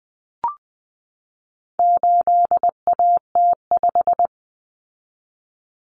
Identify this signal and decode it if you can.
Morse code, heard as a keyed tone, about 20 wpm, 700 Hz. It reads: 8AT5
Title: Common Cut Numbers - 4-Digit 20wpm